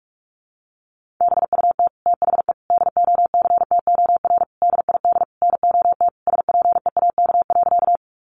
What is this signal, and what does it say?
Morse code, heard as a keyed tone, about 44 wpm, 700 Hz. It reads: BUT THE DOCTOR DID NOT SPEAK.
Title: Sentences from Top 1500 Words 45wpm